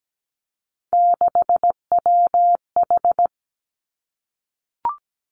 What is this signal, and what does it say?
Morse code, heard as a keyed tone, about 17 wpm, 700 Hz. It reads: 6WH
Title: Three Challenging Letter-Number Combinations 17wpm